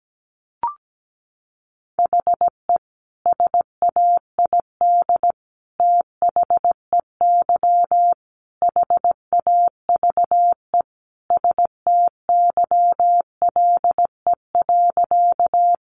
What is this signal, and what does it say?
Morse code, heard as a keyed tone, about 17 wpm, 700 Hz. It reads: HE SAID THEY HAVE STYLE.
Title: Sentences from Top 800 Words 17wpm